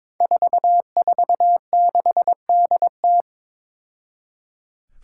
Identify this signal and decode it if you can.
Morse code, heard as a keyed tone, about 22 wpm, 700 Hz. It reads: 446DT
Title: Five Challenging Letter-Number Combinations 22wpm